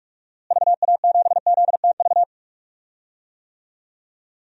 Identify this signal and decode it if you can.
Morse code, heard as a keyed tone, about 46 wpm, 700 Hz. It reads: VA7ZTV
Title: DXCC-100 Call Signs - 6 Characters - Repeated 2x - 45wpm